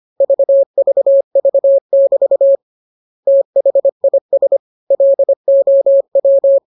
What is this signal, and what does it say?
Morse code, heard as a keyed tone, about 25 wpm, 550 Hz. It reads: VVV<BT> THIS LOW